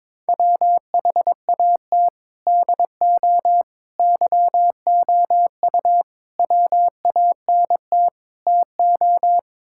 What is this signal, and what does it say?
Morse code, heard as a keyed tone, about 22 wpm, 700 Hz. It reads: WHAT DO YOU WANT TO
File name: practice / 22wpm / Build A Phrase - 5-Word Phrases - 22wpm